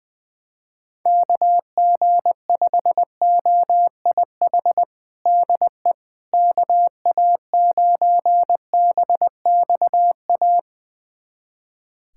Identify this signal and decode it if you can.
Morse code, heard as a keyed tone, about 20 wpm, 700 Hz. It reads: KG5OIH DE KA9BXA
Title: USA Callsign DE Callsign - 6 Characters - 20wpm